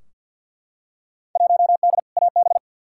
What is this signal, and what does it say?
Morse code, heard as a keyed tone, about 51 wpm, 700 Hz. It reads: 1D AB
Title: Field Day Exchanges Encore - 50wpm